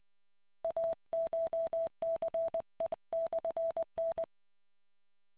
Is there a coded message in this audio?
Yes: Morse code from 0.6 to 4.2 s, heard as a keyed tone, about 20 wpm, 660 Hz. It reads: ACHCE/N